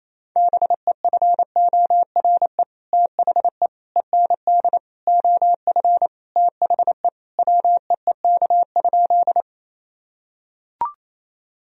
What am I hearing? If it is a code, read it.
Morse code, heard as a keyed tone, about 28 wpm, 700 Hz. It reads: BEFORE THE END OF THE WEEK?
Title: Sentences from Top 1000 Words - Review Part 2 - 28wpm